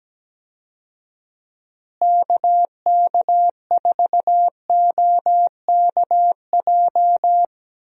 Morse code, heard as a keyed tone, about 17 wpm, 700 Hz. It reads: KK4OKJ
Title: USA Call Signs - 6 Characters - Repeated 2x - 17wpm